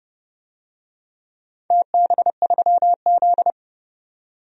Morse code, heard as a keyed tone, about 31 wpm, 700 Hz. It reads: TB3Z
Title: DXCC-100 Call Signs - 4 Characters - 30wpm